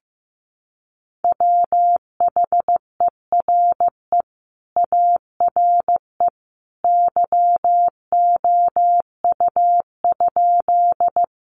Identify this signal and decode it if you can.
Morse code, heard as a keyed tone, about 15 wpm, 700 Hz. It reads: WHERE ARE YOU?
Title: Sentences from Top 200 Words 15wpm